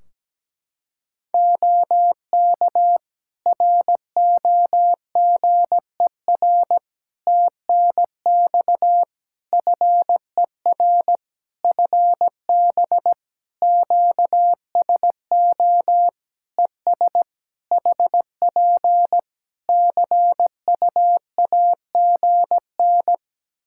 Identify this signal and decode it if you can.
Morse code, heard as a keyed tone, about 17 wpm, 700 Hz. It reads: OK ROGER TNX FER FB QSO ES HP CUAGN
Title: Practice QSO - Exchange 4 - Ending - 17wpm